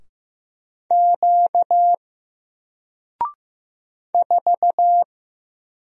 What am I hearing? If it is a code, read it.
Morse code, heard as a keyed tone, about 15 wpm, 700 Hz. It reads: Q 4
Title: Single Letter-Number 15wpm